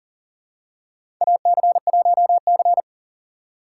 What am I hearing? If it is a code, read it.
Morse code, heard as a keyed tone, about 40 wpm, 700 Hz. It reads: AC1C
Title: USA Call Signs - 2x Character Spacing - 40wpm